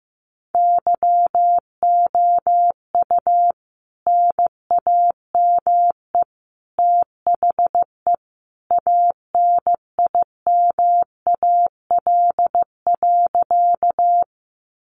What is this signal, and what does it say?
Morse code, heard as a keyed tone, about 15 wpm, 700 Hz. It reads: YOU NAME THE ANIMAL.
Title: A Sound Of Thunder 15wpm